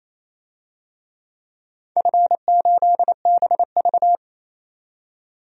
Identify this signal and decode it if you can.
Morse code, heard as a keyed tone, about 28 wpm, 700 Hz. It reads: F8BV
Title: DXCC-100 Call Signs - 4 Characters - 28wpm